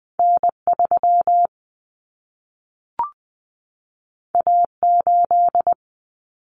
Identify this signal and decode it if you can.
Morse code, heard as a keyed tone, about 20 wpm, 700 Hz. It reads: N3 A8
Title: Common Cut Numbers - 2-Digit 20wpm